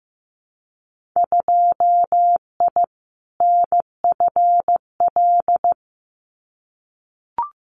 Morse code, heard as a keyed tone, about 15 wpm, 700 Hz. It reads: 2I NFL
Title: Winter Field Day Exchanges 15wpm